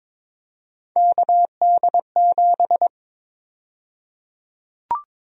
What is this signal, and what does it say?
Morse code, heard as a keyed tone, about 22 wpm, 700 Hz. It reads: KD7
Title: USA Call Sign Prefixes - Repeated 2x - 22wpm